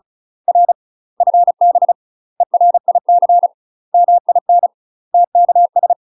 Morse code, heard as a keyed tone, about 34 wpm, 700 Hz. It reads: R FB ERIC MIN TKS
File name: UK MASTER QSOs TRAINING / 35WPM / UK MASTER QSOs 0034